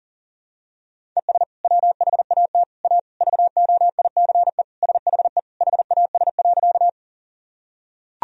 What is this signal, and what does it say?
Morse code, heard as a keyed tone, about 37 wpm, 700 Hz. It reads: ES WHAT A VOICE SHE HAS.
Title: Sentences from Top 500 Words and Abbreviations 40wpm